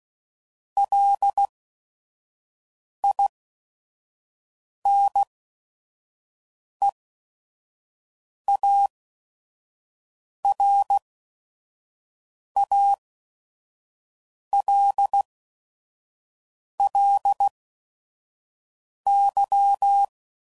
Morse code, heard as a keyed tone, about 16 wpm, spaced as for 5 wpm, 800 Hz. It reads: LINEARALLY
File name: CW Code Groups @5WPM / Technical Subjects / XL